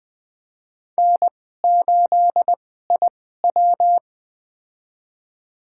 Morse code, heard as a keyed tone, about 20 wpm, spaced as for 15 wpm, 700 Hz. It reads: N8IW
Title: USA Call Signs - 4 Characters - 2x Character Spacing - 20wpm